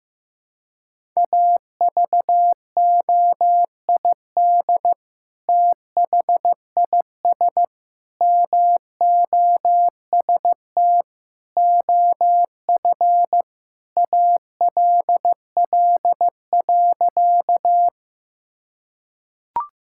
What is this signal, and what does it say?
Morse code, heard as a keyed tone, about 16 wpm, 700 Hz. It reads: AVOID THIS MOST OF ALL.
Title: Sentences from Top 1000 Words - Review Part 6 - 15wpm